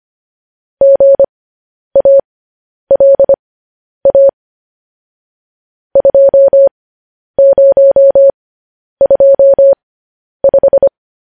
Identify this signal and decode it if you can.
Morse code, heard as a keyed tone, about 25 wpm, spaced as for 10 wpm, 550 Hz. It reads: GALA 2025
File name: podcast / News headlines-1x-25x10